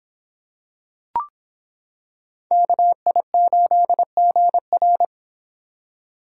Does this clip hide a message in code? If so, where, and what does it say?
Yes: Morse code from 2.5 to 5.0 s, heard as a keyed tone, about 26 wpm, 700 Hz. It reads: KI8GR